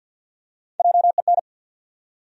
Morse code, heard as a keyed tone, about 51 wpm, 700 Hz. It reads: JEN
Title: USA Common Names - Max 3 Letters - 50wpm